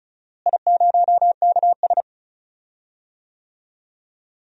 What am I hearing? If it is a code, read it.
Morse code, heard as a keyed tone, about 36 wpm, 700 Hz. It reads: I0KS
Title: DXCC-100 Call Signs - 4 Characters - Repeated 2x - 35wpm